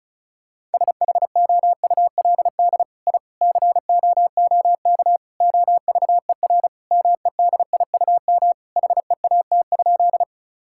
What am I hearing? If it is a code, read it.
Morse code, heard as a keyed tone, about 34 wpm, 700 Hz. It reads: SHOULD I COOK OVER MEDIUM HEAT?
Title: Sentences from Top 1000 Words - Review Part 6 - 35wpm